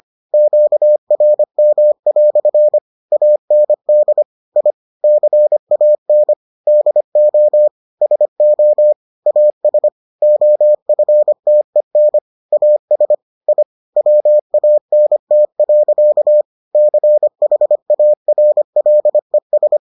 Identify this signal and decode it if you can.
Morse code, heard as a keyed tone, about 25 wpm, 600 Hz. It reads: QRM" AND I CAN DO SO AS OFTEN AS I WANT. CHARLES